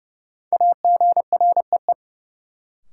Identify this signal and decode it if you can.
Morse code, heard as a keyed tone, about 31 wpm, 700 Hz. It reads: AGREE
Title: Sentences from Top 1100 Words - New Word Drill Alt - 30wpm